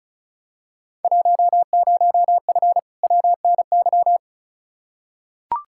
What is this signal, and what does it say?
Morse code, heard as a keyed tone, about 34 wpm, 700 Hz. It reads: 10F WNY
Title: Field Day Exchanges Encore - 35wpm